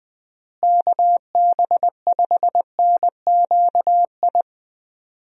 Morse code, heard as a keyed tone, about 21 wpm, 700 Hz. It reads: KB5NQI
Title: USA Call Signs - 6 Characters - Repeated 2x - 20wpm